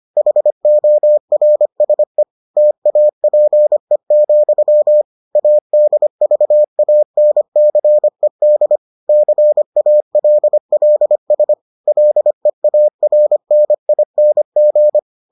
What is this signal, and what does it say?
Morse code, heard as a keyed tone, about 25 wpm, 600 Hz. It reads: HORSE TAPE, ADVANCED CALLS LEARNING